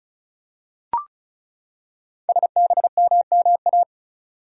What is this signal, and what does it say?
Morse code, heard as a keyed tone, about 36 wpm, 700 Hz. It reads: SBMMA
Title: Code Groups - 5 Letters - 35wpm